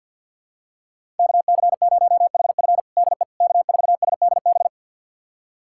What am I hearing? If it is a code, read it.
Morse code, heard as a keyed tone, about 51 wpm, 700 Hz. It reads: KC0HF DE K4SDB
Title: USA Callsign DE Callsign - 5 Characters - 50wpm